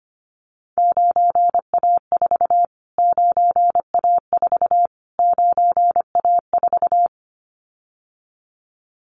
Morse code, heard as a keyed tone, about 25 wpm, 700 Hz. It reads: 9A4 9A4 9A4
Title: Callsign Prefixes for Top 25 DXCC - Repeated 3x - 25wpm